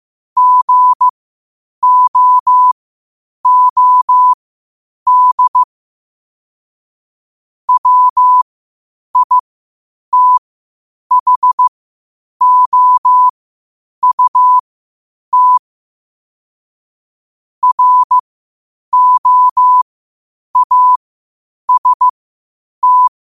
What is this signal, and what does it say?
Morse code, heard as a keyed tone, about 15 wpm, spaced as for 8 wpm, 1000 Hz. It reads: GOOD WITHOUT ROAST
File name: qotdfiles / 8wpm